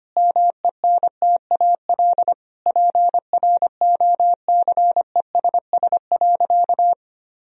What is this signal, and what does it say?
Morse code, heard as a keyed tone, about 25 wpm, 700 Hz. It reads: MENTAL PROCESS.
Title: Sentences from Top 1000 Words - Review Part 4 - 25wpm